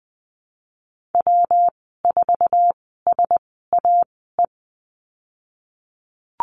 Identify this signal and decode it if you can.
Morse code, heard as a keyed tone, about 20 wpm, spaced as for 15 wpm, 700 Hz. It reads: W4SAE
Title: 161 Lesson 26g - Letter g - Call Signs 20wpm